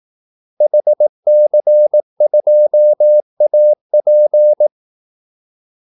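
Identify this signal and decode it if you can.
Morse code, heard as a keyed tone, about 18 wpm, 600 Hz. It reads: HC2AP
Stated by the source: Morse call-sign practice at 18 wpm.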